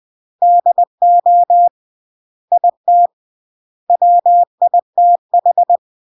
Morse code, heard as a keyed tone, about 20 wpm, 700 Hz. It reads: DO IT WITH
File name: CW / jodorowsky / 82 commandments 20-20